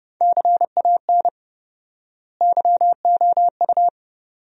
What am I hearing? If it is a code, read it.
Morse code, heard as a keyed tone, about 30 wpm, 700 Hz. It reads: CAN YOU
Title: 4x Word-Spacing - Sentences from Top 200 Words - 30wpm